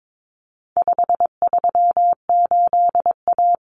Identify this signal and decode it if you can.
Morse code, heard as a keyed tone, about 22 wpm, 700 Hz. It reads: 538A
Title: Common Cut Numbers - 4-Digit 22wpm